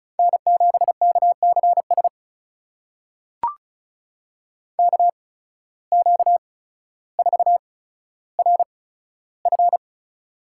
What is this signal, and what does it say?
Morse code, heard as a keyed tone, about 36 wpm, spaced as for 15 wpm, 700 Hz. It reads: N7KCS KQ4RF
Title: USA Call Signs - 5 Characters - 8x Character Spacing - 35wpm